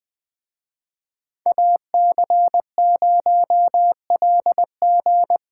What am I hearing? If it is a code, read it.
Morse code, heard as a keyed tone, about 24 wpm, 700 Hz. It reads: AC0LG
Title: Speed Racing - USA 2x2 Call Signs 20wpm